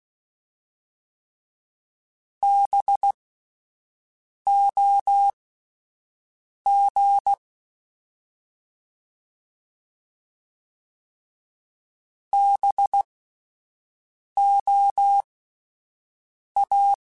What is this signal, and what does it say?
Morse code, heard as a keyed tone, about 16 wpm, spaced as for 5 wpm, 800 Hz. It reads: BOG BOA
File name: Spaced Alphas / Spaced Alphas mp3s / Bs sp @10